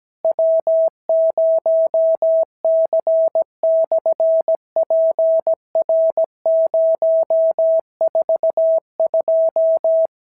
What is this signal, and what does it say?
Morse code, heard as a keyed tone, about 17 wpm, 650 Hz. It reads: W0C/PR042
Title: SOTA QSO Practice 17wpm